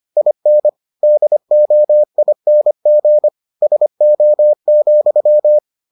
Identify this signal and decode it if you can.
Morse code, heard as a keyed tone, about 25 wpm, 600 Hz. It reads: IN DOING SO,